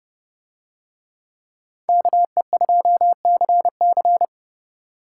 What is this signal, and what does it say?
Morse code, heard as a keyed tone, about 29 wpm, 700 Hz. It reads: KE2CC